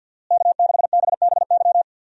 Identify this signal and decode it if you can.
Morse code, heard as a keyed tone, about 49 wpm, 700 Hz. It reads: K6BBY
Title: USA Call Signs - 5 Characters - Repeated 2x - 50wpm